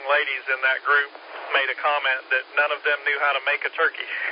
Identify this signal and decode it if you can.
Emergency Alert System (SAME) tones, heard as harsh buzzing bursts, with speech. It is heard from 0.2 to 4.1 s.